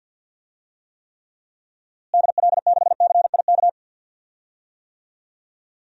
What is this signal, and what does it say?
Morse code, heard as a keyed tone, about 46 wpm, 700 Hz. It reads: DL6CIK
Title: DXCC-100 Call Signs - 6 Characters - Repeated 3x - 50wpm